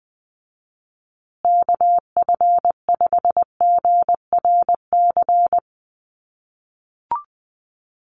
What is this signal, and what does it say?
Morse code, heard as a keyed tone, about 20 wpm, 700 Hz. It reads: KF5GRC
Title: USA Call Signs Repeated 2x 20wpm